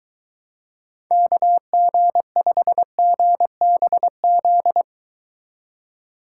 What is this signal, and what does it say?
Morse code, heard as a keyed tone, about 22 wpm, 700 Hz. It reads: KG5GBZ